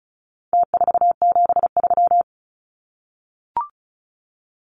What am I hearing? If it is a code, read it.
Morse code, heard as a keyed tone, about 34 wpm, 700 Hz. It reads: T473
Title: Numbers - Contest Style - 4-Digit 35wpm